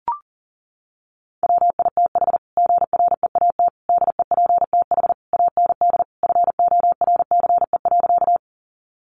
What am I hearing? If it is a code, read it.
Morse code, heard as a keyed tone, about 40 wpm, 700 Hz. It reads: WITH GREAT DEPTH AND FORCE.
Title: Sentences from Top 1200 Words 40wpm